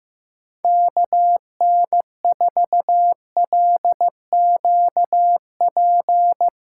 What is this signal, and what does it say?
Morse code, heard as a keyed tone, about 15 wpm, 700 Hz. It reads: KN4LQP
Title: USA Callsign DE Callsign 15wpm